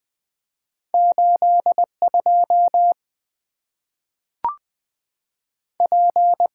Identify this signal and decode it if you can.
Morse code, heard as a keyed tone, about 20 wpm, 700 Hz. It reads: 8 2 P
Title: Two Letter-Number Combinations - 20wpm